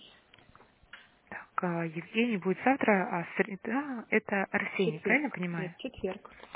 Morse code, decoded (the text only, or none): none